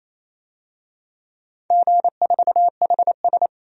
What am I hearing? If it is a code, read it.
Morse code, heard as a keyed tone, about 28 wpm, 700 Hz. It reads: G4HS
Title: DXCC-100 Call Signs - 4 Characters - 28wpm